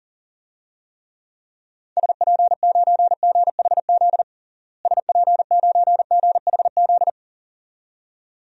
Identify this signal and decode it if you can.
Morse code, heard as a keyed tone, about 40 wpm, 700 Hz. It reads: SP9GHZ SP9GHZ
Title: DXCC-100 Call Signs - 6 Characters - Repeated 2x - 40wpm